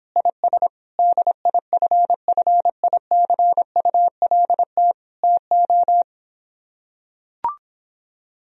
Morse code, heard as a keyed tone, about 26 wpm, 700 Hz. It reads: IS DIFFICULT TO